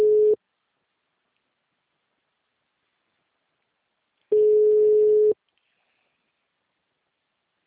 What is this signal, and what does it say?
Ringback tone (Europe, 425 Hz).